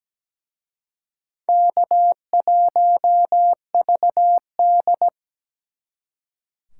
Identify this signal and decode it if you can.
Morse code, heard as a keyed tone, about 20 wpm, 700 Hz. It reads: K1VD
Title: Speed Racing - USA 1x2 Call Signs 17wpm